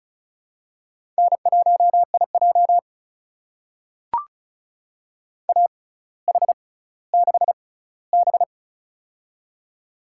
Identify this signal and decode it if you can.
Morse code, heard as a keyed tone, about 34 wpm, spaced as for 18 wpm, 700 Hz. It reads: N1IJ AH6B
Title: USA Call Signs - 4 Characters - 6x Character Spacing - 35wpm